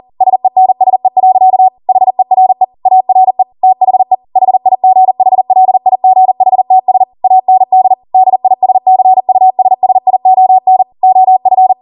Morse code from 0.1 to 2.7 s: SENSE. HERE